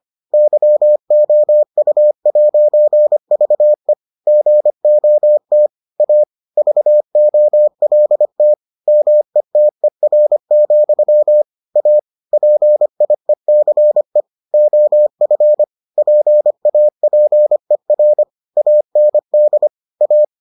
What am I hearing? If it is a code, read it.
Morse code, heard as a keyed tone, about 25 wpm, 600 Hz. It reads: YOU'VE GOT A VOLT METER, A PIECE OF PAPER AND A